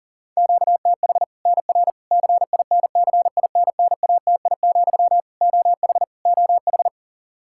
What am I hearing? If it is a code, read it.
Morse code, heard as a keyed tone, about 37 wpm, 700 Hz. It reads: QTH NR CINCINNATI, OH OH
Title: Practice QSO - Exchange 1 - Beginning - 40wpm